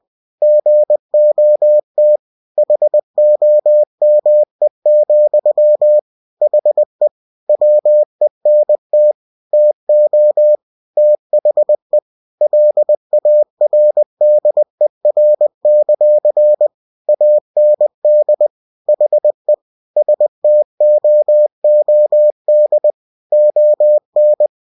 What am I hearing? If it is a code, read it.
Morse code, heard as a keyed tone, about 20 wpm, 600 Hz. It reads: GOT HOME, HE WENT TO THE LARDER; AND HE STOOD ON
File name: cw-books / winnie-the-pooh / winnie-20wpm-page-0037